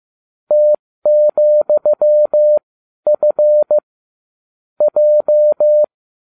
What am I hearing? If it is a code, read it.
Morse code, heard as a keyed tone, about 15 wpm, spaced as for 12 wpm, 600 Hz. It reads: T,F J